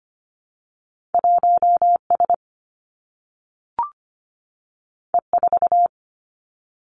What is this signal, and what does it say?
Morse code, heard as a keyed tone, about 25 wpm, 700 Hz. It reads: 1S E4
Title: Two Challenging Letter-Number Combinations 25wpm